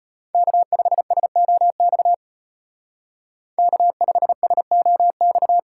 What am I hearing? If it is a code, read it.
Morse code, heard as a keyed tone, about 31 wpm, 700 Hz. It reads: K5SOX K5SOX